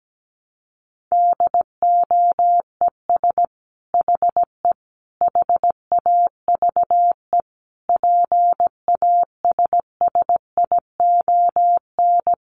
Morse code, heard as a keyed tone, about 17 wpm, 700 Hz. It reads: DOES HE HAVE PASSION
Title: Sentences from Top 1300 Words 17wpm